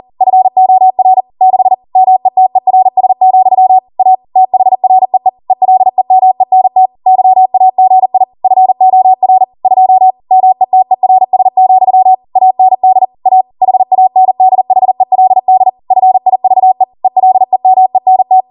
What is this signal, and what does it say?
Morse code, heard as a keyed tone, about 40 wpm, 750 Hz. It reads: FOR 6 METERS, A THREE ELEMENT YAGI FOR 2 METERS, AND A HANDHELD FIVE ELEMENT